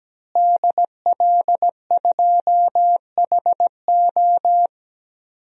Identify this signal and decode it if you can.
Morse code, heard as a keyed tone, about 17 wpm, 700 Hz. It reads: DL2HO
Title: DXCC-100 Call Signs - 5 Characters - Repeated 2x - 17wpm